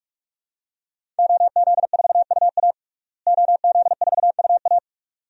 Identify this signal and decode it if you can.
Morse code, heard as a keyed tone, about 43 wpm, 700 Hz. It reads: OZ4UA OZ4UA
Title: DXCC-100 Call Signs - 5 Characters - Repeated 2x - 45wpm